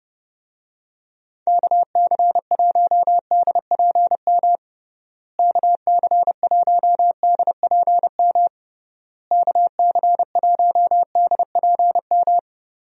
Morse code, heard as a keyed tone, about 29 wpm, 700 Hz. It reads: KC1DPM KC1DPM KC1DPM
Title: USA Call Signs - 6 Characters - Repeated 3x - 30wpm